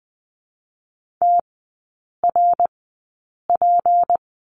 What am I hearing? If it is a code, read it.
Morse code, heard as a keyed tone, about 20 wpm, spaced as for 8 wpm, 700 Hz. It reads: TRP
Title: 152 Lesson 25e - Letter p - Three Characters 20wpm